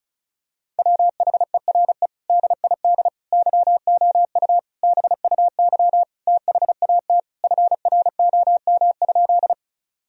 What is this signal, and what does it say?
Morse code, heard as a keyed tone, about 34 wpm, 700 Hz. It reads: WHERE DID YOU BUY THAT FROM?
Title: Sentences from Top 1000 Words - Review Part 4 - 35wpm